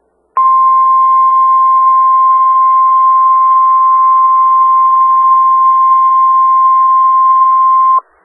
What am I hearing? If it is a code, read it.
Emergency Alert System (SAME) tones (the 1050 Hz attention tone). It is heard from 0.3 to 8.0 s.
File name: Skywarn/2013/2013-03-18-Newton County WA4ASI Repeater